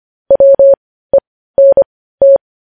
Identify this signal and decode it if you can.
Morse code, heard as a keyed tone, about 25 wpm, spaced as for 15 wpm, 550 Hz. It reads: WENT
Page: news headlines in morse code at 15 wpm